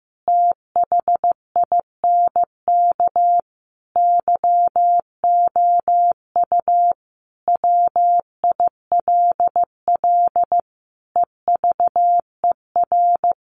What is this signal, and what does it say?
Morse code, heard as a keyed tone, about 15 wpm, 700 Hz. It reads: THINK YOU WILL EVER
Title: Sentences from Top 1300 Words 15wpm